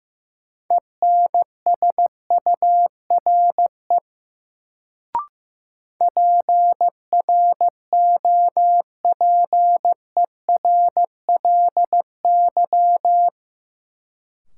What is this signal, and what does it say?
Morse code, heard as a keyed tone, about 15 wpm, 700 Hz. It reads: ENSURE PROPERLY
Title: Sentences from Top 1100 Words - New Word Drill - 15wpm